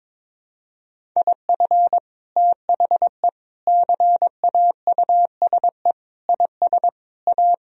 Morse code, heard as a keyed tone, about 22 wpm, 700 Hz. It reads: IF THE CAUSE IS A
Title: Sentences from Top 1000 Words - Review Part 1 - 22wpm